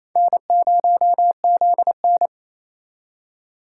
Morse code, heard as a keyed tone, about 36 wpm, 700 Hz. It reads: N0ZN